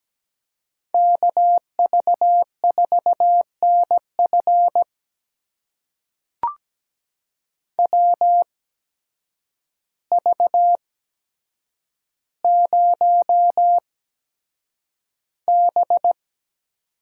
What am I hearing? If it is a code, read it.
Morse code, heard as a keyed tone, about 17 wpm, spaced as for 8 wpm, 700 Hz. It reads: KV4NF WV0B
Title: USA Call Signs - 8x Character Spacing - 17wpm